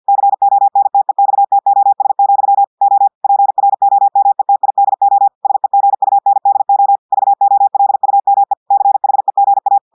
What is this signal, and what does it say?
Morse code, heard as a keyed tone, about 50 wpm, 800 Hz. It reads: CONTEXTOS, O PROMETIDO SEGUNDO VOLUME CHEGA
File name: audios / portugais / portugais texte en clair 1 vitesse 50 mots minutes